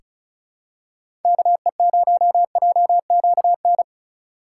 Morse code, heard as a keyed tone, about 36 wpm, 700 Hz. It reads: KE0JQN
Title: USA Call Signs - 6 Characters - 8x Character Spacing - 35wpm